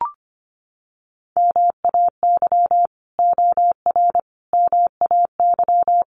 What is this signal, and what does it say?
Morse code, heard as a keyed tone, about 25 wpm, 700 Hz. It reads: MAY OR MAY
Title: Sets of 5 Words 25wpm